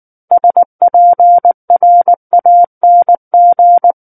Morse code, heard as a keyed tone, about 19 wpm, 700 Hz. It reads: SPRANG
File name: Book One / 10-Chapter 10 - 19 WPM